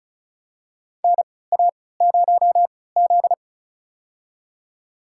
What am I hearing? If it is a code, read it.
Morse code, heard as a keyed tone, about 36 wpm, spaced as for 20 wpm, 700 Hz. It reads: NA0Z
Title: USA Call Signs - 4 Characters - 3x Character Spacing - 35wpm